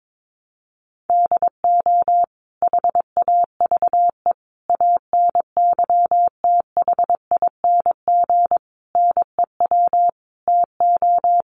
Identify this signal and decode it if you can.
Morse code, heard as a keyed tone, about 22 wpm, 700 Hz. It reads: DO HAVE ANYTHING NEW TO
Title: Sentences from Top 1300 Words 22wpm